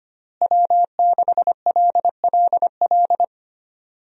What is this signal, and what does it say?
Morse code, heard as a keyed tone, about 25 wpm, 700 Hz. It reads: W6LLL
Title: USA Call Signs - 5 Characters - Repeated 3x - 25wpm